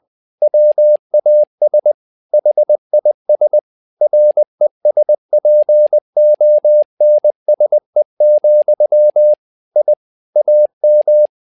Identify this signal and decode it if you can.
Morse code, heard as a keyed tone, about 20 wpm, 600 Hz. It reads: WAS HIS RESPONSE, I AM
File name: learning center / tower trouble / tower trouble 20wpm